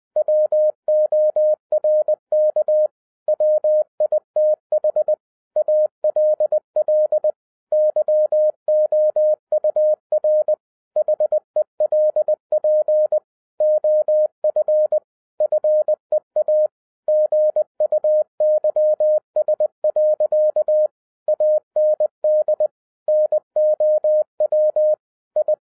Morse code, heard as a keyed tone, about 20 wpm, 610 Hz. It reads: WORK WITH ALL YOUR HELP OF FEA GUYS. AND NOW I
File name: 2012-01-mp3 / n16-THROUGH EIGHT MONTHS-20wpm